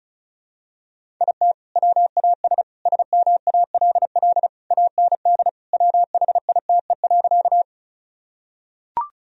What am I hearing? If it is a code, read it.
Morse code, heard as a keyed tone, about 34 wpm, 700 Hz. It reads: IT WAS SMALL AND WHITE.
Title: Sentences from Top 1000 Words - Review Part 1 - 35wpm